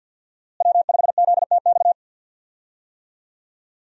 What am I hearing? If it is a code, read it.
Morse code, heard as a keyed tone, about 53 wpm, 700 Hz. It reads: W5ZTX
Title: USA Call Signs - 5 Characters - Repeated 2x - 50wpm